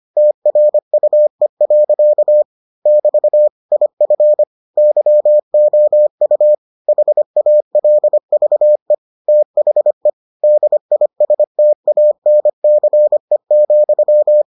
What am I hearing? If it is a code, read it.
Morse code, heard as a keyed tone, about 25 wpm, 600 Hz. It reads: TRUE. <BT> IF YOU HALVE THE DISTANCE,